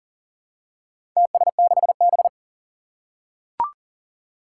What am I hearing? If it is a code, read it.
Morse code, heard as a keyed tone, about 38 wpm, 700 Hz. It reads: TS6B
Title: Four Challenging Letter-Number Combinations 40wpm